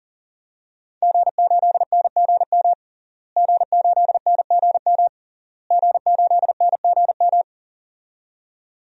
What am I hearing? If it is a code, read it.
Morse code, heard as a keyed tone, about 37 wpm, 700 Hz. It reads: G8NGM G8NGM G8NGM
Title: DXCC-100 Call Signs - 5 Characters - Repeated 3x - 40wpm